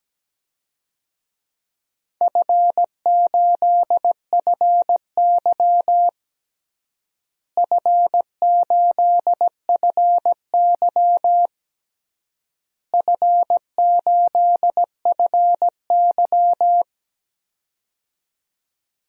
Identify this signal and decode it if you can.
Morse code, heard as a keyed tone, about 17 wpm, 700 Hz. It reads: F8FY F8FY F8FY
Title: DXCC-100 Call Signs - 4 Characters - Repeated 3x - 17wpm